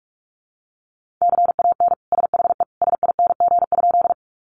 Morse code, heard as a keyed tone, about 46 wpm, 700 Hz. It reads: CAN SHE SING?
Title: Sentences from Top 400 Words 45wpm